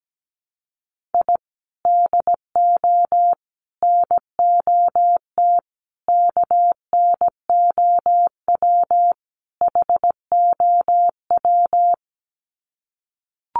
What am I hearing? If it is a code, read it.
Morse code, heard as a keyed tone, about 17 wpm, 700 Hz. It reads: I DO NOT KNOW HOW
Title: Sets of 5 Words 17wpm